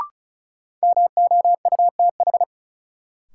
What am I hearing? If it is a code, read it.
Morse code, heard as a keyed tone, about 34 wpm, 700 Hz. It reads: MOUTH
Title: Sentences from Top 1500 Words - New Word Drill - 35wpm